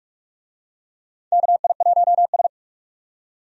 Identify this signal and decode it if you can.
Morse code, heard as a keyed tone, about 46 wpm, 700 Hz. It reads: KI1S
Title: USA Call Signs - 4 Characters - 45wpm